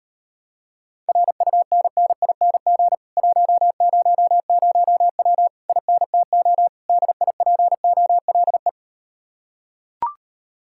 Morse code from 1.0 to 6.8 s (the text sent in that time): RUNNING 100W INTO